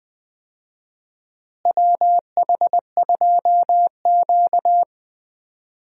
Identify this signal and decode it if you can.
Morse code, heard as a keyed tone, about 20 wpm, 700 Hz. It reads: WH2Q
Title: USA Call Signs - 4 Characters - Repeated 3x - 20wpm